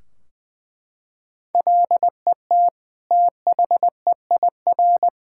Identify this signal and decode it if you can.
Morse code, heard as a keyed tone, about 19 wpm, 700 Hz. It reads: LET THEIR
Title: Sentences from Top 1600 Words 20wpm